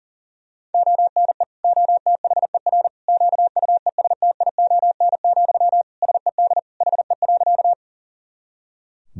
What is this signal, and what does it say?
Morse code, heard as a keyed tone, about 40 wpm, 700 Hz. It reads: ONE OTHER QUESTION, SED HE.
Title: Sentences from Top 400 Words and Abbreviations 40wpm